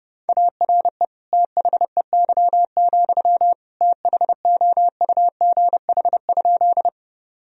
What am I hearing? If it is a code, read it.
Morse code, heard as a keyed tone, about 30 wpm, 700 Hz. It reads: ARE THEY, THOUGH?
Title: Sentences from Top 1000 Words - Review Part 2 - 30wpm